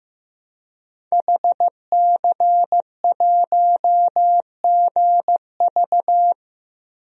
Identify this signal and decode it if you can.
Morse code, heard as a keyed tone, about 15 wpm, 700 Hz. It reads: HC1GV
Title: DXCC-100 Call Signs - 5 Characters - 15wpm